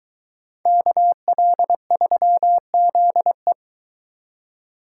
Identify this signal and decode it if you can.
Morse code, heard as a keyed tone, about 22 wpm, 700 Hz. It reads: KL3ZE